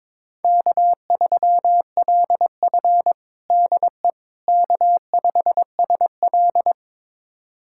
Morse code, heard as a keyed tone, about 22 wpm, 700 Hz. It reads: K3LF DE K5SL
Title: USA Callsign DE Callsign - 4 Characters - 22wpm